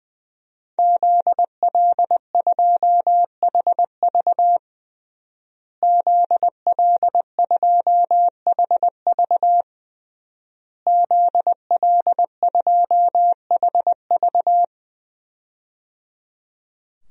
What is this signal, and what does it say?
Morse code, heard as a keyed tone, about 20 wpm, 700 Hz. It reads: ZL2HV ZL2HV ZL2HV
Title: DXCC-100 Call Signs - 5 Characters - Repeated 3x - 20wpm